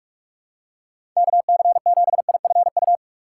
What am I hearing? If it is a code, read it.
Morse code, heard as a keyed tone, about 46 wpm, 700 Hz. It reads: KC7IFU
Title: USA Call Signs - 6 Characters - 6x Character Spacing - 45wpm